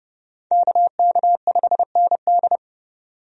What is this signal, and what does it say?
Morse code, heard as a keyed tone, about 30 wpm, 700 Hz. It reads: KK5ND
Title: USA Call Signs - 5 Characters - 30wpm